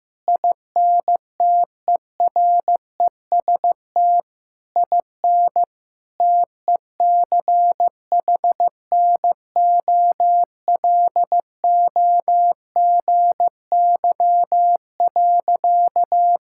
Morse code, heard as a keyed tone, about 15 wpm, 700 Hz. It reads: INTEREST IN TECHNOLOGY.
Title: Sentences from Top 1000 Words - Review Part 5 - 15wpm